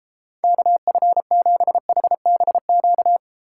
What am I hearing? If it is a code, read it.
Morse code, heard as a keyed tone, about 32 wpm, 700 Hz. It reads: KF7HBQ